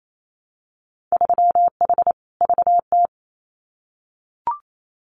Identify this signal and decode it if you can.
Morse code, heard as a keyed tone, about 28 wpm, 700 Hz. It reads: 3H VT
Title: Winter Field Day Exchanges Encore 28wpm